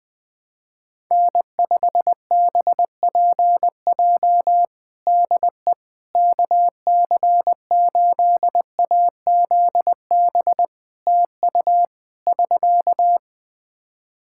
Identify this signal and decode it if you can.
Morse code, heard as a keyed tone, about 20 wpm, 700 Hz. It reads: N5BPJ DE KC8AZB TU <SK>
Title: K1USN SST QSO Practice 20wpm